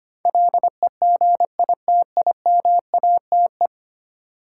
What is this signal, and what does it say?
Morse code, heard as a keyed tone, about 25 wpm, 700 Hz. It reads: LEGITIMATE
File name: Ten Letter Words - 25wpm